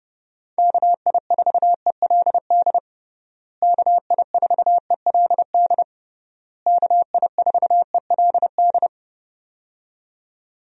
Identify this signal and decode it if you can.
Morse code, heard as a keyed tone, about 30 wpm, 700 Hz. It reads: KI4ELD KI4ELD KI4ELD
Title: USA Call Signs - 6 Characters - Repeated 3x - 30wpm